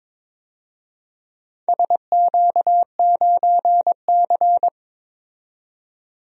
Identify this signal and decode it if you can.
Morse code, heard as a keyed tone, about 22 wpm, 700 Hz. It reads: SQ9C
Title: DXCC-100 Call Signs - 4 Characters - 22wpm